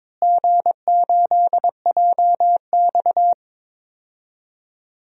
Morse code, heard as a keyed tone, about 22 wpm, 700 Hz. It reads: G8JX
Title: DXCC-100 Call Signs - 4 Characters - Repeated 2x - 22wpm